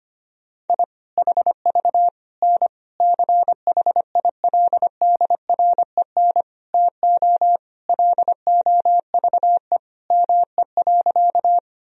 Morse code, heard as a keyed tone, about 25 wpm, 700 Hz. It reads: I HV N CHILDREN TO LOVE ME.
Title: Sentences from Top 400 Words and Abbreviations 25wpm